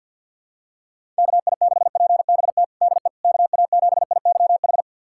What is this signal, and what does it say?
Morse code, heard as a keyed tone, about 52 wpm, 700 Hz. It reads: KI6PBT DE KA7IYH
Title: USA Callsign DE Callsign - 6 Characters - 50wpm